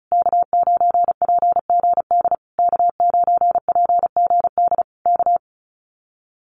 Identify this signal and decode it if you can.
Morse code, heard as a keyed tone, about 34 wpm, 700 Hz. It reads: K9PGD K9PGD K
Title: Common QSO Phrases Encore 35wpm